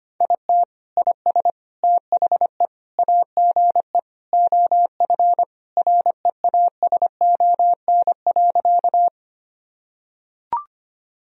Morse code, heard as a keyed tone, about 25 wpm, 700 Hz. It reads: IT IS THE AGE OF REASON.
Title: Sentences from Top 1000 Words - Review Part 3 - 25wpm